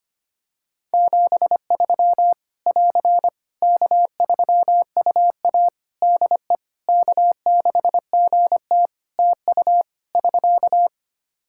Morse code, heard as a keyed tone, about 25 wpm, 700 Hz. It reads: 73 <AR> K3UA DE K6GT TU <SK>
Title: Practice QSO - Exchange 4 - Ending - 25wpm